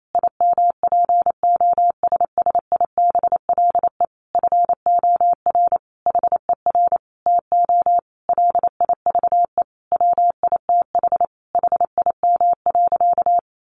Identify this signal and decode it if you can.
Morse code, heard as a keyed tone, about 28 wpm, 700 Hz. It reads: IMPOSSIBLE FOR HER TO LIVE WITH HIM.
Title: Sentences from Top 1100 Words 28wpm